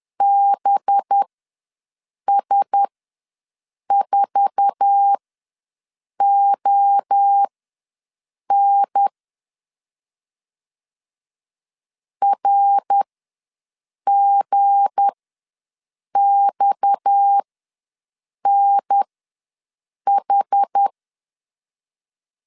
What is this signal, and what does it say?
Morse code, heard as a keyed tone, about 10 wpm, spaced as for 6 wpm, 800 Hz. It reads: BS4ON RGXNH
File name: LEZIONI / 8 YJ45 / 8 YJ4 ws 10wpm 4020